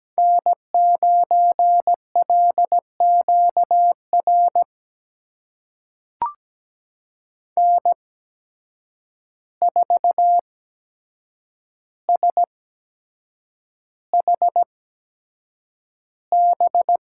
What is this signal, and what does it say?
Morse code, heard as a keyed tone, about 17 wpm, spaced as for 7 wpm, 700 Hz. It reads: N9LQR N4SHB
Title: USA Call Signs - 8x Character Spacing - 17wpm